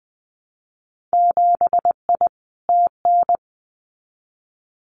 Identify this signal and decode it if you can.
Morse code, heard as a keyed tone, about 20 wpm, 700 Hz. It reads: 7I TN
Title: Winter Field Day Exchanges Encore 20wpm